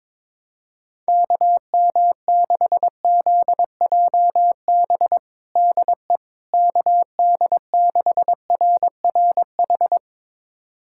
Morse code, heard as a keyed tone, about 21 wpm, 700 Hz. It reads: KM6ZJB DE KD6RRH
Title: USA Callsign DE Callsign 22wpm